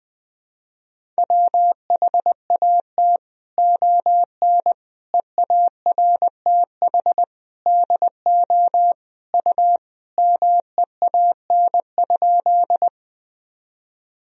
Morse code, heard as a keyed tone, about 21 wpm, 700 Hz. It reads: WHAT ON EARTH DO U MEAN?
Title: Sentences from Top 200 Words and Abbreviations 20wpm